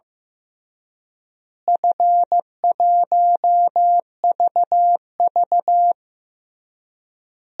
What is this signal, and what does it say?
Morse code, heard as a keyed tone, about 15 wpm, 700 Hz. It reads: F1VV
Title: DXCC-100 Call Signs - 4 Characters - Repeated 3x - 15wpm